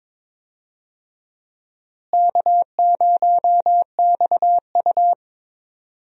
Morse code, heard as a keyed tone, about 22 wpm, 700 Hz. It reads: K0XU
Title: USA Call Signs - 4 Characters - Repeated 2x - 22wpm